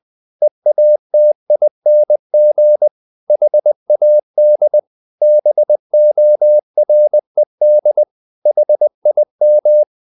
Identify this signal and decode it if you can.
Morse code, heard as a keyed tone, about 20 wpm, 600 Hz. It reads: EATING HAD BORED HIM
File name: the old man and the sea / the old man and the sea part 1 20wpm